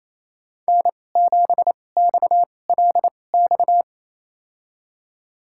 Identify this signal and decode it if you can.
Morse code, heard as a keyed tone, about 28 wpm, spaced as for 20 wpm, 700 Hz. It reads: N7XLX
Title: USA Call Signs - 2x Character Spacing - 28wpm